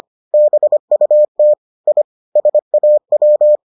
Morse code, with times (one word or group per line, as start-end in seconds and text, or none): 0.3-1.5 BUT
1.9-2.0 I
2.4-3.6 SAW